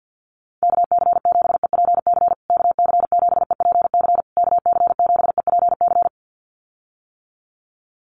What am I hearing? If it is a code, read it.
Morse code, heard as a keyed tone, about 50 wpm, 700 Hz. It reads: KC7EPC KC7EPC KC7EPC
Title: USA Call Signs Repeated 3x 50wpm